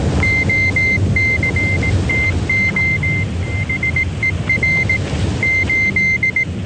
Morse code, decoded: OCTOSER 8